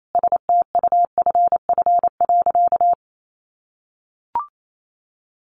Morse code, heard as a keyed tone, about 28 wpm, 700 Hz. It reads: STUFF.
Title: Sentences from Top 1400 Words 28wpm